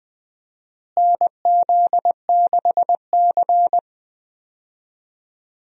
Morse code, heard as a keyed tone, about 20 wpm, 700 Hz. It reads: NZ6C
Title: USA Call Signs - 4 Characters - Repeated 2x - 20wpm